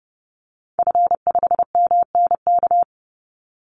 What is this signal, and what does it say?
Morse code, heard as a keyed tone, about 30 wpm, 700 Hz. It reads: F5MNK
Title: Call Signs - 30wpm